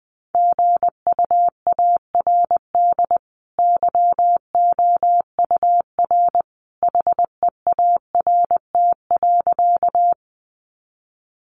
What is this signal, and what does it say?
Morse code, heard as a keyed tone, about 20 wpm, 700 Hz. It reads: GUARD YOUR HEART.
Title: Sentences from Top 1200 Words 20wpm